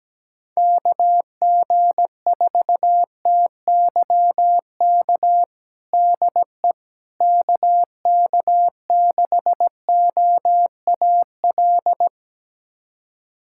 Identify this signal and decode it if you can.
Morse code, heard as a keyed tone, about 17 wpm, 700 Hz. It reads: KG4TYK DE KK6OAL
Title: USA Callsign DE Callsign - 6 Characters - 17wpm